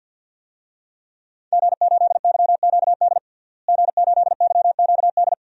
Morse code, heard as a keyed tone, about 51 wpm, 700 Hz. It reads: G8YQD G8YQD
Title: DXCC-100 Call Signs - 5 Characters - Repeated 2x - 50wpm